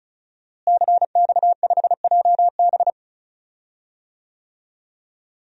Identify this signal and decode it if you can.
Morse code, heard as a keyed tone, about 34 wpm, 700 Hz. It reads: CX5JB
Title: DXCC-100 Call Signs - 5 Characters - Repeated 3x - 35wpm